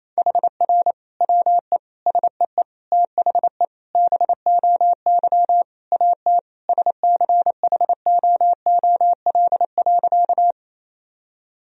Morse code, heard as a keyed tone, about 27 wpm, 700 Hz. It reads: HR WE SEE THE BOY AT SCHOOL.
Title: Sentences from Top 300 Words and Abbreviations 28wpm